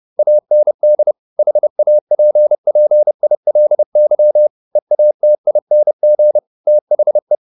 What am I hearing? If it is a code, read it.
Morse code, heard as a keyed tone, about 30 wpm, 600 Hz. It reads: AND HAPPILY EATING THE
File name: the old man and the sea / the old man and the sea 1 30wpm